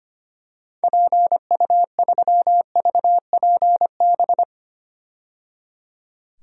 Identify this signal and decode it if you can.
Morse code, heard as a keyed tone, about 25 wpm, 690 Hz. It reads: PU3VPB
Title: DXCC-100 Call Signs - 6 Characters - 25wpm